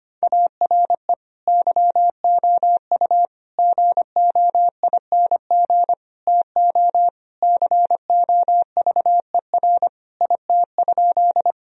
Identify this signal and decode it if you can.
Morse code, heard as a keyed tone, about 25 wpm, 700 Hz. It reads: ARE YOU GOING TO COVER IT?
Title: Sentences from Top 1000 Words - Review Part 1 - 25wpm